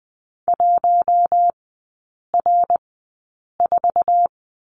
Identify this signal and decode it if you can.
Morse code, heard as a keyed tone, about 20 wpm, spaced as for 8 wpm, 700 Hz. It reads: 1R4
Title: 053 Lesson 10e - Letter r - Three Characters 20wpm